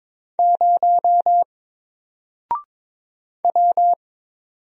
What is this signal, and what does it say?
Morse code, heard as a keyed tone, about 22 wpm, 700 Hz. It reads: 0 W
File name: Single-Letter-Number-22wpm